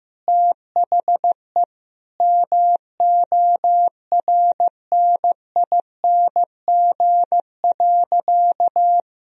Morse code, heard as a keyed tone, about 15 wpm, 700 Hz. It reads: THE MORNING.
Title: Sentences from Top 1000 Words - Review Part 2 - 15wpm